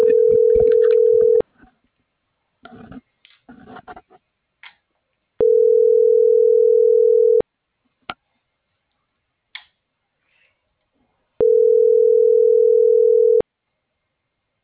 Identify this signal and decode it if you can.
Ringback tone (North America), 440 + 480 Hz.